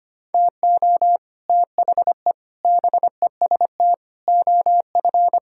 Morse code, heard as a keyed tone, about 25 wpm, 700 Hz. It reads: TO THE BEST OF
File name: Sets of 9 Words - 25wpm